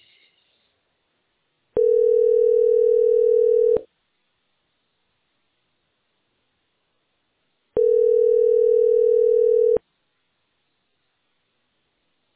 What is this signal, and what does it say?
Ringback tone (North America), 440 + 480 Hz.
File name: 7d43bec8-a07e-40f7-9eea